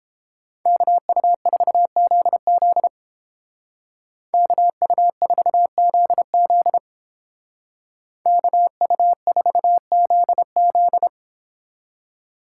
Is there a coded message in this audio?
Yes: Morse code from 0.7 to 11.1 s, heard as a keyed tone, about 29 wpm, 700 Hz. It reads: KU4ZZ KU4ZZ KU4ZZ